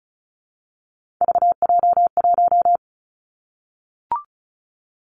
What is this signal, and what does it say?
Morse code, heard as a keyed tone, about 36 wpm, 700 Hz. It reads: VJ1
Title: Three Challenging Letter-Number Combinations 35wpm